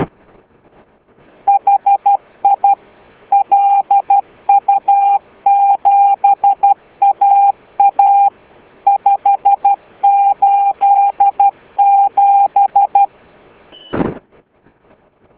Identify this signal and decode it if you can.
Morse code, heard as a keyed tone, about 12 wpm, 800 Hz. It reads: HI LU7AA 587